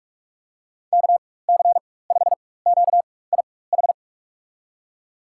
Morse code, heard as a keyed tone, about 44 wpm, spaced as for 21 wpm, 690 Hz. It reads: KC5QIH
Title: USA Call Signs - 4x Character Spacing - 45wpm